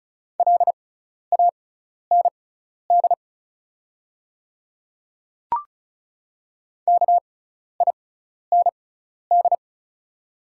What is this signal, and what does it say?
Morse code, heard as a keyed tone, about 35 wpm, spaced as for 12 wpm, 690 Hz. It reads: LAND KIND
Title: 6x Character Spacing - Top 300 Words - 35wpm